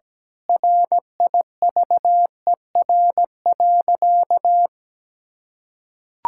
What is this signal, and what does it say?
Morse code, heard as a keyed tone, about 17 wpm, 700 Hz. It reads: RIVER.
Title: Sentences from Top 1000 Words - Review Part 6 - 17wpm